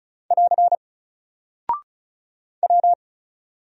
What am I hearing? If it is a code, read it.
Morse code, heard as a keyed tone, about 36 wpm, 700 Hz. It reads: <AR> W
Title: Single Character (Letter:Number:Punctuation:Prosign) 35wpm